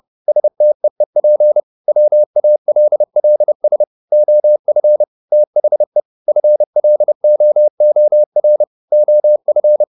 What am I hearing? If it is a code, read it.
Morse code, heard as a keyed tone, about 30 wpm, 600 Hz. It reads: STEEP WALLS OF THE FLOOR OF
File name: the old man and the sea / the old man and the sea 1 30wpm